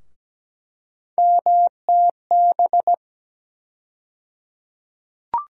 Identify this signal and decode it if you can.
Morse code, heard as a keyed tone, about 17 wpm, 700 Hz. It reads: MTB
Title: USA Call Sign Postfixes - 3 Letters 17wpm